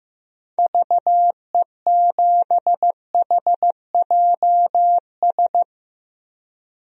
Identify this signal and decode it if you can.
Morse code, heard as a keyed tone, about 15 wpm, 700 Hz. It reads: VE7HJS
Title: DXCC-100 Call Signs - 6 Characters - Repeated 2x - 15wpm